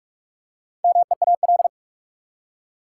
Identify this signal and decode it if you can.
Morse code, heard as a keyed tone, about 42 wpm, 700 Hz. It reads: MEAL
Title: Four Letter Words - 45wpm